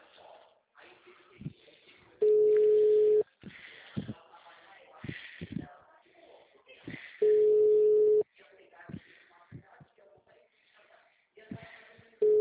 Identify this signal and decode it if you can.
Ringback tone (Europe, 425 Hz), with speech.
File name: out- -2007- - - .623